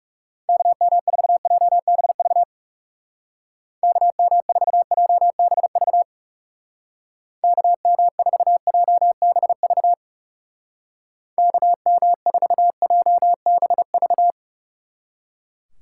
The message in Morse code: KM4JBV KM4JBV KM4JBV KM4JBV

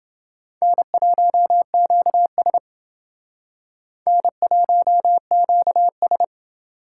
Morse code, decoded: N1QS N1QS

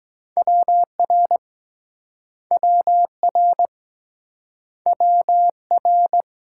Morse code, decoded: WR WR WR

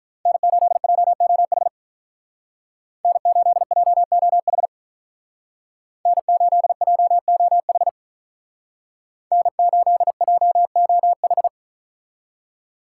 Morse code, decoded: N8JOH N8JOH N8JOH N8JOH

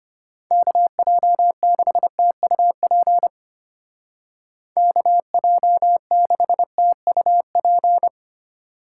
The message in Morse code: KJ6TUP KJ6TUP

